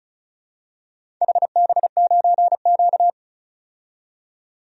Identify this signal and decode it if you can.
Morse code, heard as a keyed tone, about 36 wpm, 690 Hz. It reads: HB9Q
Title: DXCC-100 Call Signs - 4 Characters - 35wpm